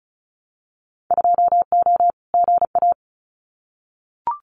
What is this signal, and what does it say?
Morse code, heard as a keyed tone, about 36 wpm, 700 Hz. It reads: 2O GA
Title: Winter Field Day Exchanges Encore 35wpm